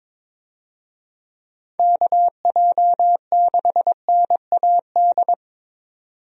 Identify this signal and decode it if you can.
Morse code, heard as a keyed tone, about 22 wpm, 700 Hz. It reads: KJ6NAD
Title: USA Call Signs - 6 Characters - 22wpm